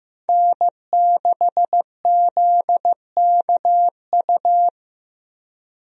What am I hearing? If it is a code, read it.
Morse code, heard as a keyed tone, about 15 wpm, 700 Hz. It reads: N6ZKU
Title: USA Call Signs - 5 Characters - Repeated 2x - 15wpm